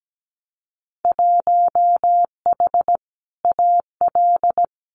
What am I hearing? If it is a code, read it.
Morse code, heard as a keyed tone, about 17 wpm, 700 Hz. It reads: 1H AL
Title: Winter Field Day Exchanges 17wpm